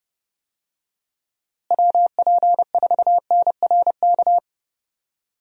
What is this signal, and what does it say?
Morse code, heard as a keyed tone, about 32 wpm, 700 Hz. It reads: WP4NRK
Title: USA Call Signs - 6 Characters - Repeated 3x - 30wpm